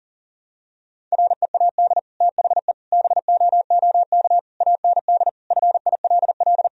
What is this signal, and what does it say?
Morse code, heard as a keyed tone, about 40 wpm, 700 Hz. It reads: READ THE BOOK AND FILL
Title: Sentences from Top 1000 Words - Review Part 3 - 40wpm